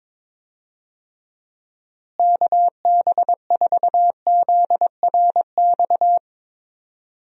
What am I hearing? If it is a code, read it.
Morse code, heard as a keyed tone, about 22 wpm, 700 Hz. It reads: KB4ZRX
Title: USA Call Signs - 6 Characters - Repeated 2x - 22wpm